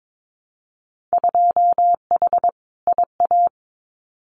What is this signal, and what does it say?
Morse code, heard as a keyed tone, about 22 wpm, 700 Hz. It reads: 2H IA
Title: Winter Field Day Exchanges Encore 22wpm